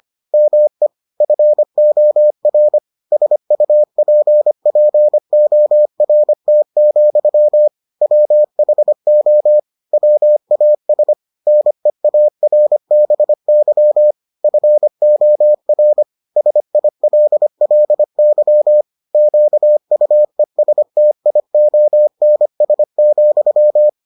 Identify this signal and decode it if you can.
Morse code, heard as a keyed tone, about 25 wpm, 600 Hz. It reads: ME FOR SUPPORT, WHO WAS NEARBY FOR SILLY QUESTIONS,